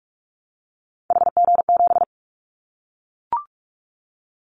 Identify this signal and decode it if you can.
Morse code, heard as a keyed tone, about 46 wpm, 700 Hz. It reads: HG7
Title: Callsign Prefixes for Top 25 DXCC - 45wpm